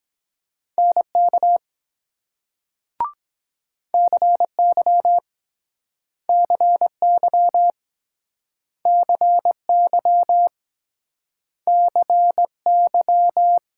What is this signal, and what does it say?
Morse code, heard as a keyed tone, about 24 wpm, 700 Hz. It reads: NK CY CY CY CY